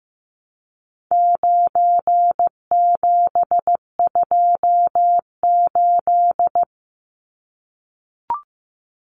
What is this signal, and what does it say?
Morse code, heard as a keyed tone, about 15 wpm, 700 Hz. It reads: 9728
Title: Numbers - 4-Digit 15wpm